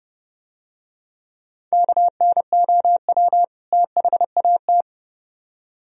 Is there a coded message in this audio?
Yes: Morse code from 1.7 to 4.8 s, heard as a keyed tone, about 31 wpm, 700 Hz. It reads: KNOW THAT